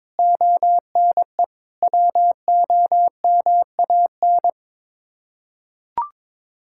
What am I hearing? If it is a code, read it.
Morse code, heard as a keyed tone, about 22 wpm, 700 Hz. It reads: ONE WOMAN
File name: practice / 22wpm / Sets of 6 Words - 22wpm